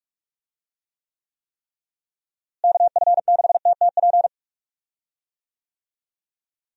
Morse code, heard as a keyed tone, about 46 wpm, 700 Hz. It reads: KF6TTP
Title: USA Call Signs - 6 Characters - Repeated 3x - 45wpm